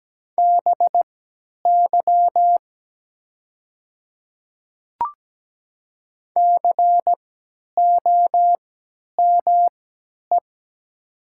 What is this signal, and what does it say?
Morse code, heard as a keyed tone, about 17 wpm, spaced as for 10 wpm, 700 Hz. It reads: BY COME
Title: 3x Character Spacing - Top 200 Words - 17wpm